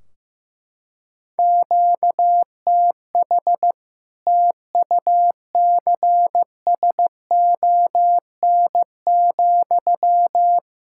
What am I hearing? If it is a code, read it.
Morse code, heard as a keyed tone, about 15 wpm, 700 Hz. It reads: QTH TUCSON,
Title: Practice QSO - Exchange 1 - Beginning - 15wpm